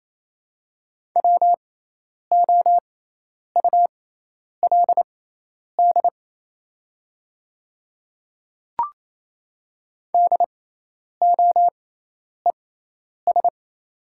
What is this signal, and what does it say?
Morse code, heard as a keyed tone, about 28 wpm, spaced as for 10 wpm, 700 Hz. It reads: WOULD DOES
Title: 6x Character Spacing - Top 300 Words - 28wpm